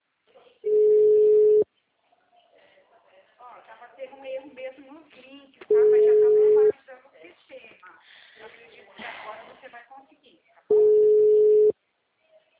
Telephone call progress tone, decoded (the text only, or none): ringback tone